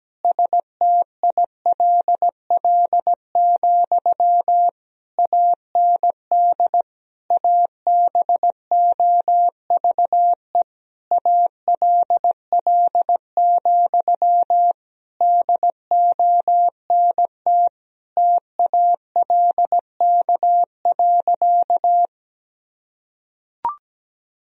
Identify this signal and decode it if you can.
Morse code, heard as a keyed tone, about 17 wpm, 700 Hz. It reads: STILL, AND ABOVE ALL, DONT TALK.
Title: Sentences from Top 1000 Words - Review Part 2 - 17wpm